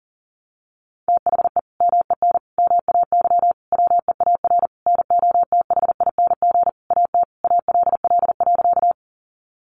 Morse code, heard as a keyed tone, about 40 wpm, 700 Hz. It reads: THE MEN MAY WEAR NOTHING AT ALL.
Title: Sentences from Top 1500 Words 40wpm